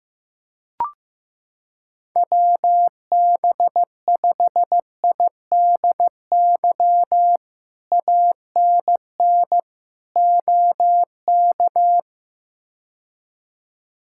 Morse code, heard as a keyed tone, about 15 wpm, 700 Hz. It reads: WB5IDY ANN OK
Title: K1USN SST QSO Practice - 15wpm